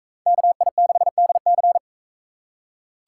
Morse code, heard as a keyed tone, about 41 wpm, 700 Hz. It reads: KI6DC